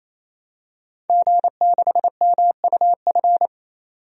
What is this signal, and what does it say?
Morse code, heard as a keyed tone, about 28 wpm, 700 Hz. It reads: G6MUF
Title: DXCC-100 Call Signs - 5 Characters - 28wpm